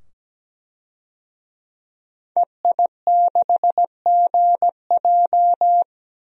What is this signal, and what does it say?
Morse code, heard as a keyed tone, about 17 wpm, 700 Hz. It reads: EI6GJ
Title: DXCC-100 Call Signs - 5 Characters - 17wpm